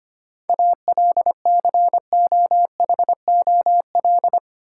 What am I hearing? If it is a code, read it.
Morse code, heard as a keyed tone, about 25 wpm, 700 Hz. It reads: ALCOHOL
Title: Sentences from Top 1400 Words - New Word Drill Alt - 25wpm